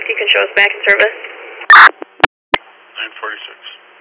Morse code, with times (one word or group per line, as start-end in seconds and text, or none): none